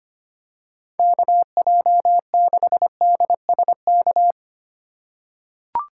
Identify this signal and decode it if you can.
Morse code, heard as a keyed tone, about 25 wpm, 700 Hz. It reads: KJ6DSK
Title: USA Call Signs - 2x Character Spacing - 25wpm